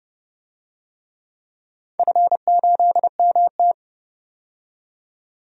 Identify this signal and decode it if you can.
Morse code, heard as a keyed tone, about 30 wpm, 700 Hz. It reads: F8MT
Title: DXCC-100 Call Signs - 4 Characters - 30wpm